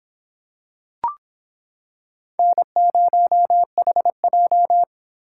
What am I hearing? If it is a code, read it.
Morse code, heard as a keyed tone, about 26 wpm, 700 Hz. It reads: N0HJ